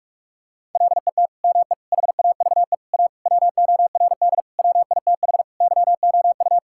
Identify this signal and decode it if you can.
Morse code, heard as a keyed tone, about 46 wpm, 700 Hz. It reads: LET ME HAVE A WORD WITH YOU
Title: Sentences from Top 1000 Words - Review Part 6 - 45wpm